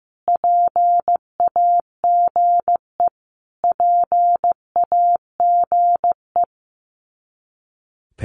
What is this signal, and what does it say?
Morse code, heard as a keyed tone, about 15 wpm, 700 Hz. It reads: PAGE PAGE
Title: Top 500 Words Repeated 15wpm